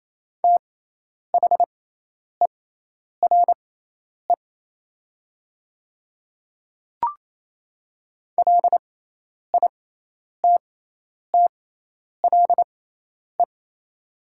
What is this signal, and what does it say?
Morse code, heard as a keyed tone, about 28 wpm, spaced as for 10 wpm, 690 Hz. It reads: THERE LITTLE
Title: 6x Character Spacing - Top 500 Words - 28wpm